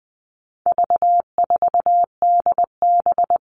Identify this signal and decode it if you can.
Morse code, heard as a keyed tone, about 20 wpm, 700 Hz. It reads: V4DB
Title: All Cut Numbers - 4-Digit 20wpm